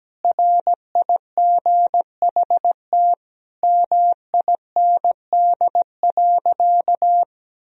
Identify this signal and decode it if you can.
Morse code, heard as a keyed tone, about 17 wpm, 700 Hz. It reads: RIGHT MIND.
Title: Sentences from Top 1000 Words - Review Part 1 - 17wpm